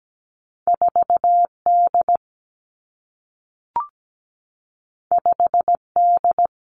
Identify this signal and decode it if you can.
Morse code, heard as a keyed tone, about 17 wpm, 700 Hz. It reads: 4 D 5 D
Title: Two Challenging Letter-Number Combinations 17wpm